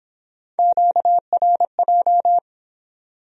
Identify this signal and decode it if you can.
Morse code, heard as a keyed tone, about 25 wpm, 700 Hz. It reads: QRJ